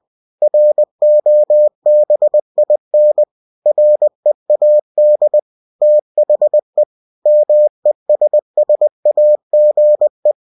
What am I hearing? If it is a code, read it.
Morse code, heard as a keyed tone, about 20 wpm, 600 Hz. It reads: ROBIN READ THE MESSAGE
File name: cw-books / winnie-the-pooh / winnie-20wpm-page-0098